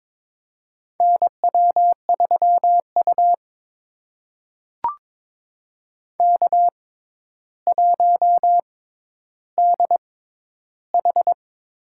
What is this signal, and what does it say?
Morse code, heard as a keyed tone, about 22 wpm, spaced as for 11 wpm, 700 Hz. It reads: NW3U K1DH
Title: USA Call Signs - 4 Characters - 6x Character Spacing - 22wpm